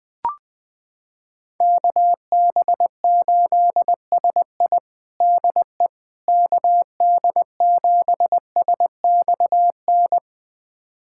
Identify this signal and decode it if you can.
Morse code, heard as a keyed tone, about 21 wpm, 700 Hz. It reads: KB8SI DE KD7SXN
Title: USA Callsign DE Callsign 20wpm